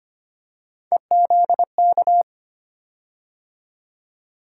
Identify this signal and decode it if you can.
Morse code, heard as a keyed tone, about 25 wpm, 700 Hz. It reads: EZK
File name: Three Letter-Number Combinations - 25wpm